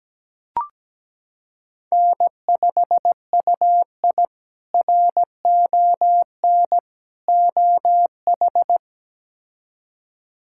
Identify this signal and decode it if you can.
Morse code, heard as a keyed tone, about 17 wpm, 700 Hz. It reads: N5UI RON OH
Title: K1USN SST QSO Practice - 17wpm